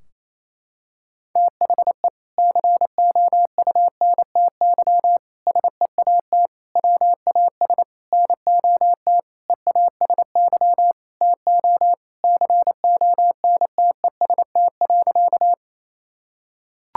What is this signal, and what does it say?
Morse code, heard as a keyed tone, about 28 wpm, 700 Hz. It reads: THE COUNTY SEAT WAS NOT EASY TO CONTEST.
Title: Sentences from Top 1600 Words 28wpm